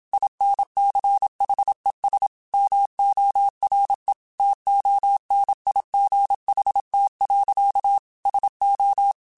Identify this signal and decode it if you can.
Morse code, heard as a keyed tone, about 27 wpm, 800 Hz. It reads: INCHES MORE TONIGHT. SO